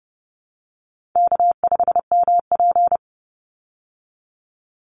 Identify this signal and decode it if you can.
Morse code, heard as a keyed tone, about 30 wpm, 700 Hz. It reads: K5MP
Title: Call Signs Repeated 3x - 30wpm